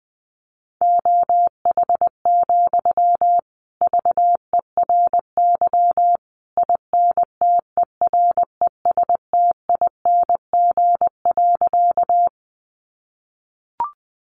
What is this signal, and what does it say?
Morse code, heard as a keyed tone, about 20 wpm, 700 Hz. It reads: OH, VERY INTERESTING.
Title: Sentences from Top 1000 Words 20wpm